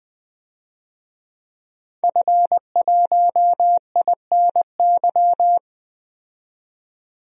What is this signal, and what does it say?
Morse code, heard as a keyed tone, about 20 wpm, 700 Hz. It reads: F1INY
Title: DXCC-100 Call Signs - 5 Characters - 20wpm